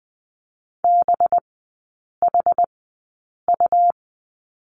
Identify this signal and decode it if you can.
Morse code, heard as a keyed tone, about 19 wpm, spaced as for 8 wpm, 700 Hz. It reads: BHU
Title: 184 Lesson 30e - Letter b - Three Characters 20wpm